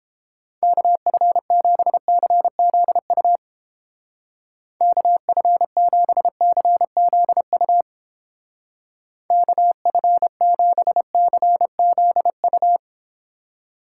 Morse code, heard as a keyed tone, about 31 wpm, 700 Hz. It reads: KF7CZU KF7CZU KF7CZU